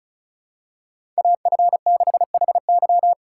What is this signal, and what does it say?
Morse code, heard as a keyed tone, about 36 wpm, 700 Hz. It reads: AF6HY
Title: USA Call Signs - 5 Characters - 4x Character Spacing - 35wpm